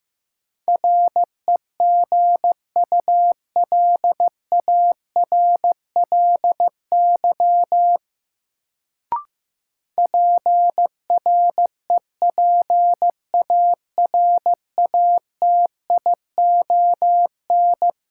Morse code, heard as a keyed tone, about 15 wpm, 700 Hz. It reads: REGULARLY PREPARATION
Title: Sentences from Top 1400 Words - New Word Drill - 15wpm